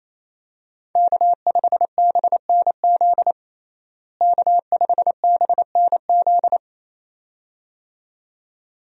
Morse code, heard as a keyed tone, about 28 wpm, 700 Hz. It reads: K5BNZ K5BNZ
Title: USA Call Signs - 5 Characters - Repeated 2x - 28wpm